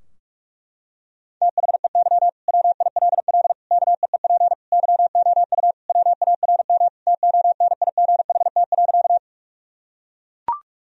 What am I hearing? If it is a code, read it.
Morse code, heard as a keyed tone, about 42 wpm, 700 Hz. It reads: THEY WILL KEEP YOU WARM TONIGHT.
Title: Sentences from Top 1000 Words - Review Part 3 - 45wpm